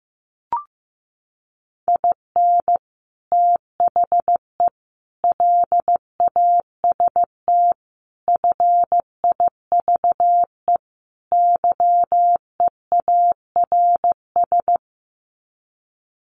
Morse code, heard as a keyed tone, about 15 wpm, 700 Hz. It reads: IN THE LAST FIVE YEARS
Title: Sets of 5 Words Encore 15wpm